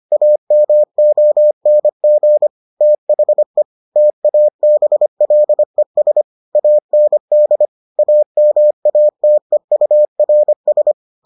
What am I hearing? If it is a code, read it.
Morse code, heard as a keyed tone, about 25 wpm, 600 Hz. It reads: AMONG THE TABLES AND AMATEURS